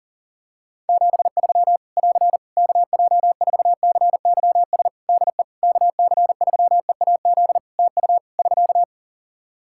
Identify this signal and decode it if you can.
Morse code, heard as a keyed tone, about 40 wpm, 700 Hz. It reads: 73 <AR> KJ4CYS DE KC3EAZ TU <SK>
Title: Practice QSO - Exchange 4 - Ending - 40wpm